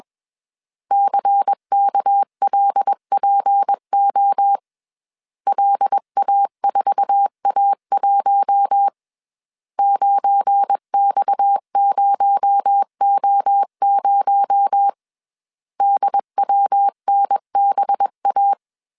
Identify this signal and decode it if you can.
Morse code, heard as a keyed tone, about 21 wpm, 800 Hz. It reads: CKLPO LA4A1 9X0O0 DWNBA